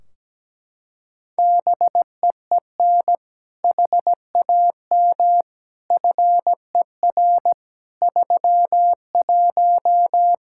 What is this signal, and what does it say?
Morse code, heard as a keyed tone, about 17 wpm, 700 Hz. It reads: BEEN HAM FER 31
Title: Practice QSO - Exchange 3 - More Info - 17wpm